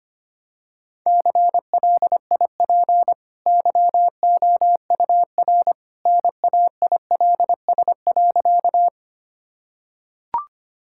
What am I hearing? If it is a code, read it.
Morse code, heard as a keyed tone, about 25 wpm, 700 Hz. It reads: CLIP YOUR NAILS.
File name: Easy Sentences with Hints - 25wpm